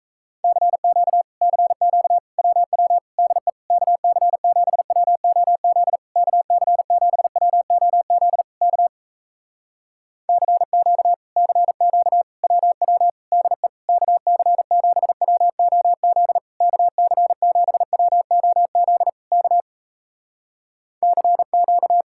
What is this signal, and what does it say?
Morse code, heard as a keyed tone, about 38 wpm, 700 Hz. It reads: CQ CQ WW DE KC7WOZ KC7WOZ K CQ CQ WW DE KC7WOZ KC7WOZ K CQ